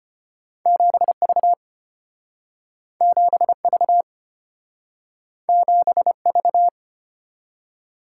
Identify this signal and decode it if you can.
Morse code, heard as a keyed tone, about 33 wpm, 700 Hz. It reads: 7V 7V 7V